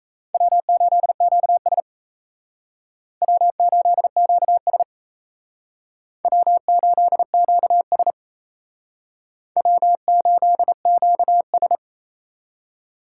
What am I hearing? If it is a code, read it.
Morse code, heard as a keyed tone, about 36 wpm, 700 Hz. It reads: W8QS W8QS W8QS W8QS